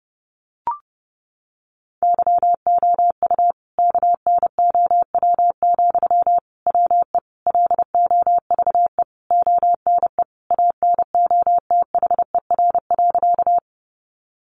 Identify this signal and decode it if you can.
Morse code, heard as a keyed tone, about 30 wpm, 700 Hz. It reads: YOU KNOW, WE LOVE ONE ANOTHER.
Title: Sentences from Top 600 Words 30wpm